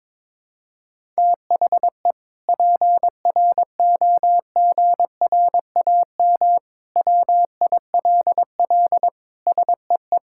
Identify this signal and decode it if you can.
Morse code, heard as a keyed tone, about 22 wpm, 700 Hz. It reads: THE PROGRAM WILL SEE
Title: Sentences from Top 1000 Words - Review Part 5 - 22wpm